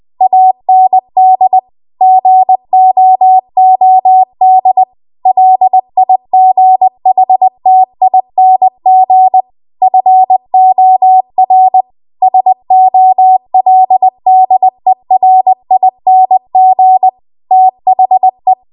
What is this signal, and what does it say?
Morse code, heard as a keyed tone, about 20 wpm, 750 Hz. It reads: AND GOOD LIGHTING FOR SOLDERING THE